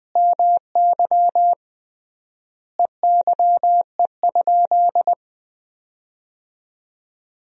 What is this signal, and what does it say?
Morse code, heard as a keyed tone, about 20 wpm, 700 Hz. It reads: MY EYE?
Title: 3x Word-Spacing - Sentences from Top 500 Words - 20wpm